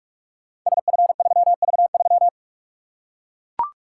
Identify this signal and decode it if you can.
Morse code, heard as a keyed tone, about 44 wpm, 700 Hz. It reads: SF3V3
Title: Code Groups - 5 Letter:Numbers - 45wpm